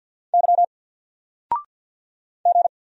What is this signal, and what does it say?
Morse code, heard as a keyed tone, about 50 wpm, 700 Hz. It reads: Y G
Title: Single Character (Letter:Number:Punctuation:Prosign) 50wpm